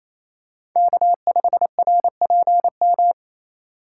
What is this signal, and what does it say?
Morse code, heard as a keyed tone, about 28 wpm, 700 Hz. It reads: K5RPM
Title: USA Call Signs - 5 Characters - 6x Character Spacing - 28wpm